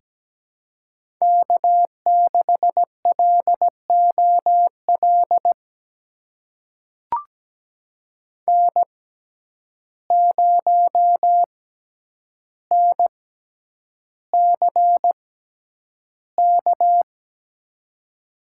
Morse code, heard as a keyed tone, about 17 wpm, spaced as for 9 wpm, 700 Hz. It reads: K6LOL N0NCK
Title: USA Call Signs - 5 Characters - 6x Character Spacing - 17wpm